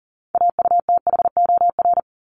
Morse code, heard as a keyed tone, about 40 wpm, 700 Hz. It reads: AUTHOR
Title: Top 1001-1250 Words 40wpm